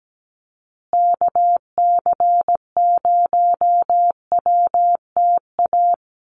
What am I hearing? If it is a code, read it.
Morse code, heard as a keyed tone, about 17 wpm, 700 Hz. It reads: KC0WTA
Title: USA Call Signs Repeated 3x 17wpm